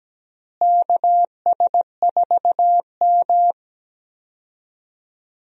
Morse code, heard as a keyed tone, about 17 wpm, 700 Hz. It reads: KS4M
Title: USA Call Signs - 4 Characters - Repeated 3x - 17wpm